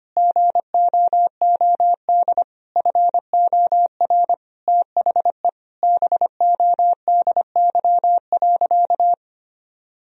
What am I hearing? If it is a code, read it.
Morse code, heard as a keyed tone, about 25 wpm, 700 Hz. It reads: GOOD FOR THE BODY.
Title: Sentences from Top 1000 Words - Review Part 4 - 25wpm